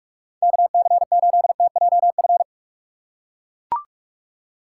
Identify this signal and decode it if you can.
Morse code, heard as a keyed tone, about 44 wpm, 700 Hz. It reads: KC8TJF
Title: USA Call Signs - 6 Characters - 4x Character Spacing - 45wpm